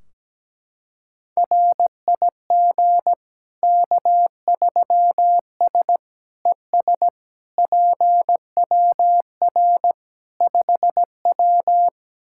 Morse code, heard as a keyed tone, about 17 wpm, 700 Hz. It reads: RIG K3S ES PWR 5W
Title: K1USN SST QSO Practice 17wpm